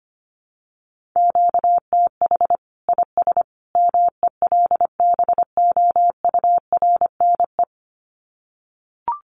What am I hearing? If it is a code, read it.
Morse code, heard as a keyed tone, about 25 wpm, 700 Hz. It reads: QTH IS MELBOURNE
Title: Common QSO Phrases 25wpm